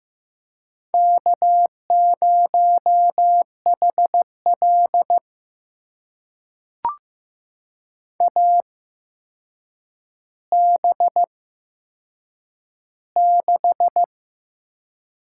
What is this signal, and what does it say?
Morse code, heard as a keyed tone, about 15 wpm, spaced as for 7 wpm, 700 Hz. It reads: K0HL AB6
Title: USA Call Signs - 4 Characters - 8x Character Spacing - 15wpm